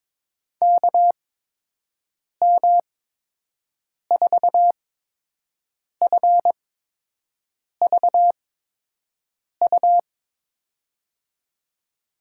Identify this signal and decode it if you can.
Morse code, heard as a keyed tone, about 21 wpm, spaced as for 6 wpm, 700 Hz. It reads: KM4FVU
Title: USA Call Signs - 6 Characters - 8x Character Spacing - 22wpm